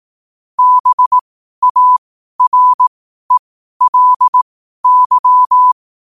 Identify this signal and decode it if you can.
Morse code, heard as a keyed tone, about 18 wpm, spaced as for 12 wpm, 1000 Hz. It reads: BARELY
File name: qotdfiles / 12wpm